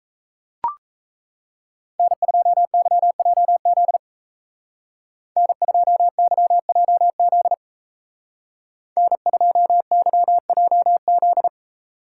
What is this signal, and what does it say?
Morse code, heard as a keyed tone, about 34 wpm, 700 Hz. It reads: N2YJZ N2YJZ N2YJZ
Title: Speed Racing - USA 1x3 Call Signs 28wpm